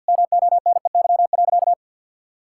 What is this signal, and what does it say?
Morse code, heard as a keyed tone, about 50 wpm, 700 Hz. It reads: MONEY.
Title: Sentences from Top 1000 Words - Review Part 6 - 50wpm